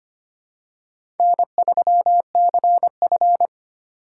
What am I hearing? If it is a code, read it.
Morse code, heard as a keyed tone, about 25 wpm, 700 Hz. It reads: N3CF
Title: USA Call Signs - 4 Characters - 8x Character Spacing - 25wpm